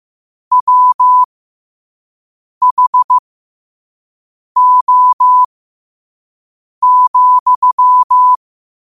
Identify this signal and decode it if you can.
Morse code, heard as a keyed tone, about 15 wpm, spaced as for 5 wpm, 1000 Hz. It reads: WHO,